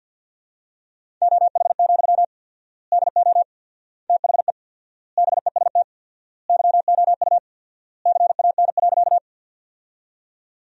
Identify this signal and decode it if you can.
Morse code, heard as a keyed tone, about 50 wpm, 700 Hz. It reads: OH, DO THE BEST YOU CAN.
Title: 4x Word-Spacing - Sentences from Top 500 Words - 50wpm